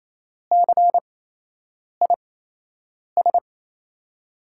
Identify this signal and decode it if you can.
Morse code, heard as a keyed tone, about 28 wpm, spaced as for 8 wpm, 700 Hz. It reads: CIS
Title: USA Call Signs - 8x Character Spacing - 28wpm